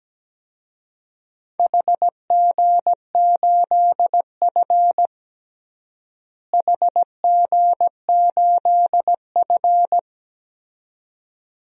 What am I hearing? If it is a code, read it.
Morse code, heard as a keyed tone, about 17 wpm, 700 Hz. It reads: HG8F HG8F
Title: DXCC-100 Call Signs - 4 Characters - Repeated 2x - 17wpm